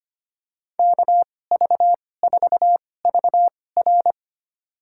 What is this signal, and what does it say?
Morse code, heard as a keyed tone, about 25 wpm, spaced as for 18 wpm, 700 Hz. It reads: KV4VR
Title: USA Call Signs - 5 Characters - 2x Character Spacing - 25wpm